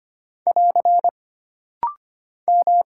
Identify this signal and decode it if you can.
Morse code, heard as a keyed tone, about 25 wpm, 700 Hz. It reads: <AR> M
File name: Single Character (Letter:Number:Punctuation:Prosign) - Rapid-Fire - 25wpm